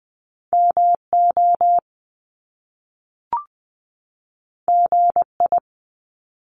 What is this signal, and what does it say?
Morse code, heard as a keyed tone, about 20 wpm, 700 Hz. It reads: MO GI
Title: Top Two Letter Combinations in Words 20wpm